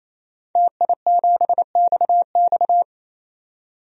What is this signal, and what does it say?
Morse code, heard as a keyed tone, about 28 wpm, 700 Hz. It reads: TI7XX
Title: DXCC-100 Call Signs - 5 Characters - 28wpm